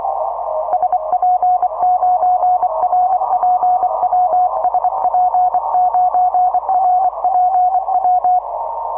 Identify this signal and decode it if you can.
Morse code, heard as a keyed tone, about 24 wpm, 720 Hz. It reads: SP9RPW SP9RPW
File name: simple-file-list / BA4SI 6MCW